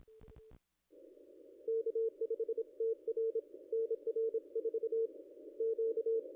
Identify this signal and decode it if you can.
Morse code, heard as a keyed tone, about 26 wpm, 450 Hz. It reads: K5TR NR 4 Q
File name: contests / ss-cw-k5tr-op-n5rz -1700